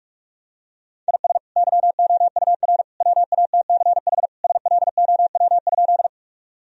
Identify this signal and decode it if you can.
Morse code, heard as a keyed tone, about 47 wpm, 700 Hz. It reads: IS YOUR WATCH SLOW?
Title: Sentences from Top 1000 Words - Review Part 2 - 45wpm